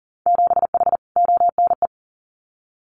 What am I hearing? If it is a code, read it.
Morse code, heard as a keyed tone, about 40 wpm, 700 Hz. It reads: 7H ONE
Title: Winter Field Day Exchanges Encore 40wpm